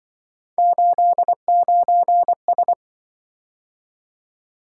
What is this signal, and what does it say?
Morse code, heard as a keyed tone, about 26 wpm, 700 Hz. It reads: 89S